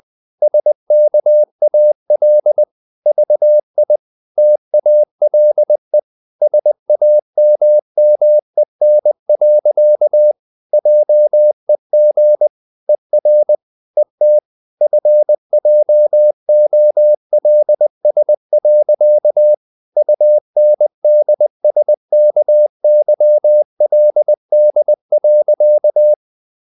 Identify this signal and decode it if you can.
Morse code, heard as a keyed tone, about 20 wpm, 600 Hz. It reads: SKAL VI TALE SAMMEN. JEG ER ET FJOLS. UNDSKYLD.
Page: CW med Gnister